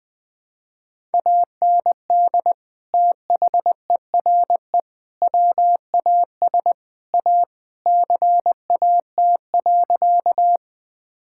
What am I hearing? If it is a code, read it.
Morse code, heard as a keyed tone, about 19 wpm, 700 Hz. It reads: AND THERE WAS A CAT.
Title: Sentences with 4-Word Phrases - Spoken Beginning - 20wpm